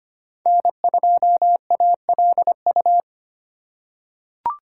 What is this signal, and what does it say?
Morse code, heard as a keyed tone, about 25 wpm, 700 Hz. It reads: N2ALU
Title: USA Call Signs - 2x Character Spacing - 25wpm